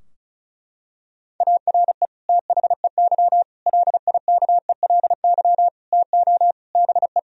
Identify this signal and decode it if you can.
Morse code, heard as a keyed tone, about 34 wpm, 700 Hz. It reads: ARE THEY LIKELY TO BE
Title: Sentences from Top 1000 Words - Review Part 4 - 35wpm